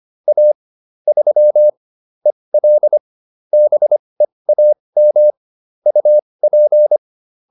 Morse code, heard as a keyed tone, about 25 wpm, spaced as for 20 wpm, 600 Hz. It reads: A 3 EL BEAM UP
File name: QSO310a 20